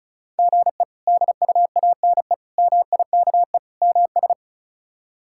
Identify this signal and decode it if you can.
Morse code, heard as a keyed tone, about 34 wpm, 700 Hz. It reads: GE DUANE MIKE MS
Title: K1USN SST QSO Practice - 35wpm